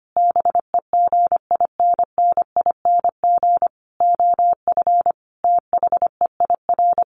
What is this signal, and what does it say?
Morse code, heard as a keyed tone, about 25 wpm, 700 Hz. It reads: BEGINNING OF THEIR
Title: Sentences from Top 900 Words 25wpm